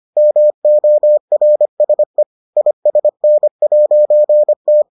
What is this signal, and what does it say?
Morse code, heard as a keyed tone, about 25 wpm, 600 Hz. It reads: MORSE ISN'T